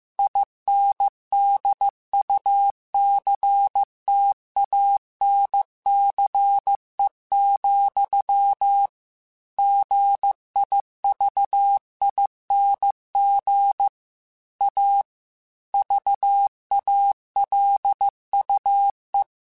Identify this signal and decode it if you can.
Morse code, heard as a keyed tone, about 15 wpm, 800 Hz. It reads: INDUCTANCE, GIVING A VALUE
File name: Technical Subjects / Technical Subjects-mp3s / XL @13